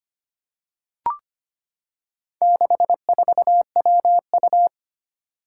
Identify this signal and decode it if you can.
Morse code, heard as a keyed tone, about 25 wpm, 700 Hz. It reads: 64WU
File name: practice / 25wpm / Four Letter-Number Combinations - 25wpm